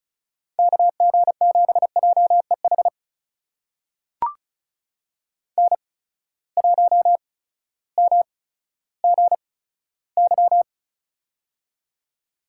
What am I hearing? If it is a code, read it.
Morse code, heard as a keyed tone, about 36 wpm, spaced as for 16 wpm, 700 Hz. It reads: KG7JEH N1MGY
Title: USA Call Signs - 8x Character Spacing - 35wpm